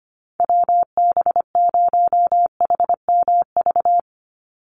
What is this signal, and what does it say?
Morse code, heard as a keyed tone, about 25 wpm, 700 Hz. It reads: WB0HMV
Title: USA Call Signs 25wpm